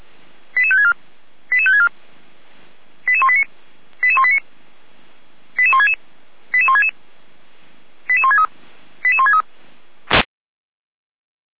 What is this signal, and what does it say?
Five-tone selective call (ZVEI), heard as quick runs of five tones, tones of about 80 ms. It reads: 88564 88564 88189 88189 88166 88166 88163 88163